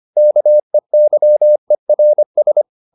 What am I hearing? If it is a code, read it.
Morse code, heard as a keyed tone, about 25 wpm, 600 Hz. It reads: KEYERS
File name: SS315 25